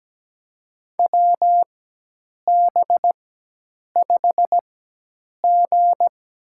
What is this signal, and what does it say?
Morse code, heard as a keyed tone, about 17 wpm, spaced as for 8 wpm, 700 Hz. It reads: WB5G
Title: USA Call Signs - 6 Characters - 4x Character Spacing - 17wpm